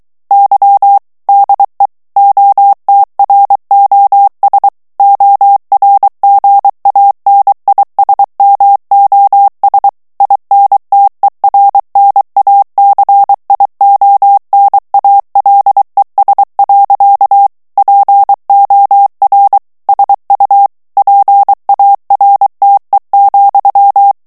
Morse code, heard as a keyed tone, about 23 wpm, 800 Hz. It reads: Y DE OTROS ORGANISMOS INTERNACIONALES. POR SU PARTE,
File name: morse mp3 / ESPAGNOL TEXTE EN CLAIR 2 VITESSE 22 MOTS MINUTES